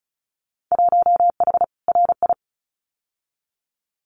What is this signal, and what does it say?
Morse code, heard as a keyed tone, about 34 wpm, 700 Hz. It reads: 1H RI
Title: Winter Field Day Exchanges 35wpm